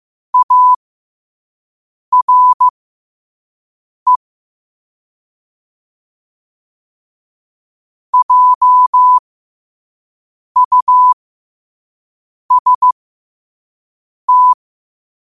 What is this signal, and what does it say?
Morse code, heard as a keyed tone, about 15 wpm, spaced as for 5 wpm, 1000 Hz. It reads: ARE JUST
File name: qotdfiles / 5wpm